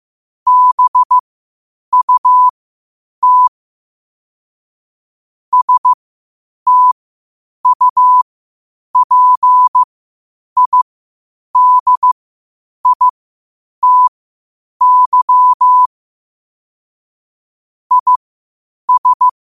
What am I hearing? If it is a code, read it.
Morse code, heard as a keyed tone, about 15 wpm, spaced as for 8 wpm, 1000 Hz. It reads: BUT STUPIDITY IS
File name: qotdfiles / 8wpm